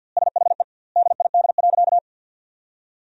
Morse code, heard as a keyed tone, about 50 wpm, 700 Hz. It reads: SHE DID.
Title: 2x Word-Spacing - Sentences from Top 100 Words - 50wpm